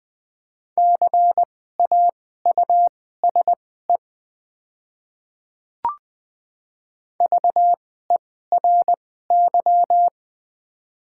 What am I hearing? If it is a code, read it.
Morse code, heard as a keyed tone, about 20 wpm, spaced as for 14 wpm, 700 Hz. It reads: CAUSE VERY
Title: 2x Character Spacing - Top 300 Words - 20wpm